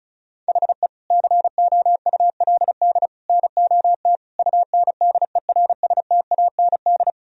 Morse code, heard as a keyed tone, about 34 wpm, 700 Hz. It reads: HE COULD NOT UNDERSTAND
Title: Sentences from Top 1000 Words - Review Part 2 - 35wpm